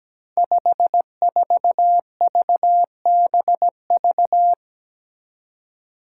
Morse code, heard as a keyed tone, about 17 wpm, 700 Hz. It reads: 54VBV
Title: Five Challenging Letter-Number Combinations 17wpm